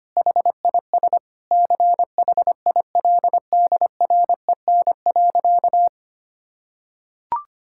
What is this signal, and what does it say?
Morse code, heard as a keyed tone, about 25 wpm, 700 Hz. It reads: HIS CHILDREN.
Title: Sentences from Top 1000 Words - Review Part 5 - 25wpm